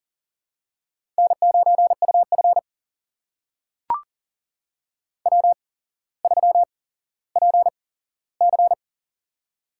Morse code, heard as a keyed tone, about 38 wpm, spaced as for 17 wpm, 700 Hz. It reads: N9UF W3PC
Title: USA Call Signs - 8x Character Spacing - 40wpm